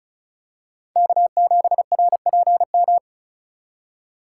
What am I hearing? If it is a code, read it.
Morse code, heard as a keyed tone, about 36 wpm, 700 Hz. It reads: K7RPM
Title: USA Call Signs - 4x Character Spacing - 35wpm